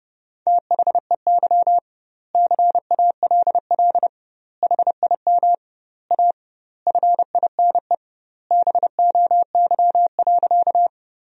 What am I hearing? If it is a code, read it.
Morse code, heard as a keyed tone, about 28 wpm, 700 Hz. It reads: THEY CALL HIM A FINE BOY.
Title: 2x Word-Spacing - Sentences from Top 500 Words - 30wpm